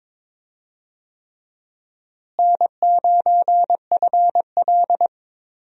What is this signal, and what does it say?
Morse code, heard as a keyed tone, about 21 wpm, 700 Hz. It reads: N9FL
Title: USA Call Signs - 4 Characters - Repeated 3x - 22wpm